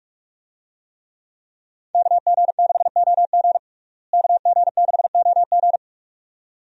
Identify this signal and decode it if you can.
Morse code, heard as a keyed tone, about 46 wpm, 700 Hz. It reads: KG6OG KG6OG
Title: USA Call Signs - 5 Characters - Repeated 2x - 45wpm